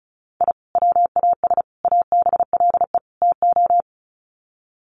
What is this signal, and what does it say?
Morse code, heard as a keyed tone, about 34 wpm, 700 Hz. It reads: I WAS ABLE TO
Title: Sets of 4 Words 35wpm